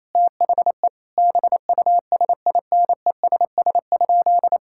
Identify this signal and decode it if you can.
Morse code, heard as a keyed tone, about 28 wpm, 700 Hz. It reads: THE BUSINESS?
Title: Sentences from Top 1000 Words - Review Part 6 - 28wpm